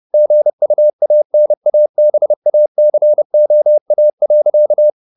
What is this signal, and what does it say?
Morse code, heard as a keyed tone, about 30 wpm, 600 Hz. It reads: GUANABACOA.